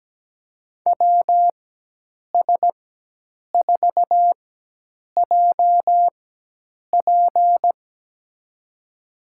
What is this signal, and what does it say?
Morse code, heard as a keyed tone, about 17 wpm, spaced as for 8 wpm, 700 Hz. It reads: WS4JP
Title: USA Call Signs - 5 Characters - 4x Character Spacing - 17wpm